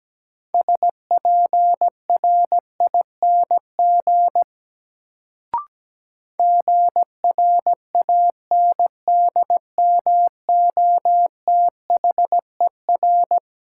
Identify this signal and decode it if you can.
Morse code, heard as a keyed tone, about 17 wpm, 700 Hz. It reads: SPRING GRANDMOTHER
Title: Sentences from Top 1200 Words - New Word Drill - 17wpm